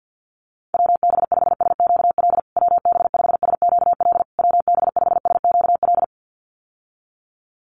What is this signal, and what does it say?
Morse code, heard as a keyed tone, about 50 wpm, 700 Hz. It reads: WB5SQL WB5SQL WB5SQL
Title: USA Call Signs Repeated 3x 50wpm